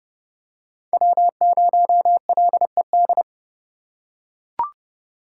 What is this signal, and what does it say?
Morse code, heard as a keyed tone, about 30 wpm, 700 Hz. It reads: W0LED